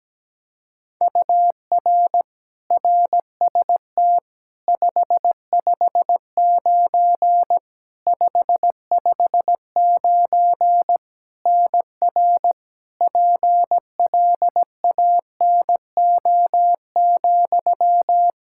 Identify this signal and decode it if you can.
Morse code, heard as a keyed tone, about 17 wpm, 700 Hz. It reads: UR RST 559 559 NR PLANO,
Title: Practice QSO - Exchange 1 - Beginning - 17wpm